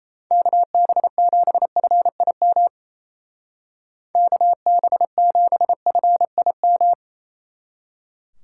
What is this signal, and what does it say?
Morse code, heard as a keyed tone, about 34 wpm, 700 Hz. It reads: KB7FIM KB7FIM
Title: Speed Racing - USA 2x3 Call Signs 28wpm